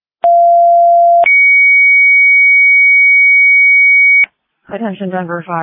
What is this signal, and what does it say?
Two-tone page, heard as a one-second tone, then a three-second tone, about 690 then 2250 Hz, with speech. It is heard from 0.2 to 4.2 s.